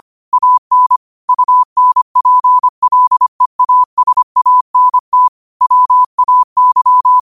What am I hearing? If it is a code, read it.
Morse code, heard as a keyed tone, about 25 wpm, 1000 Hz. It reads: AN UNPLEASANT WAY
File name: qotdfiles / 25wpm